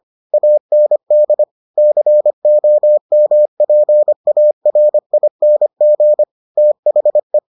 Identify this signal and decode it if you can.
Morse code, heard as a keyed tone, about 25 wpm, 600 Hz. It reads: AND COMPARING THE